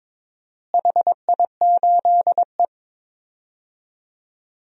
Morse code, heard as a keyed tone, about 22 wpm, 700 Hz. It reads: HI8E
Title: DXCC-100 Call Signs - 4 Characters - Repeated 2x - 22wpm